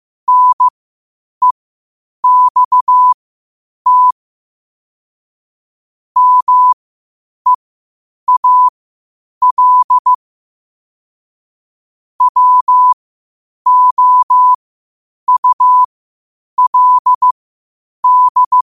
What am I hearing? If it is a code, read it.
Morse code, heard as a keyed tone, about 15 wpm, spaced as for 8 wpm, 1000 Hz. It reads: NEXT MEAL WOULD
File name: qotdfiles / 8wpm